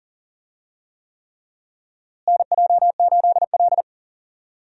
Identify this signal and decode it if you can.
Morse code, heard as a keyed tone, about 40 wpm, 700 Hz. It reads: NJ8L
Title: USA Call Signs - 4 Characters - Repeated 3x - 40wpm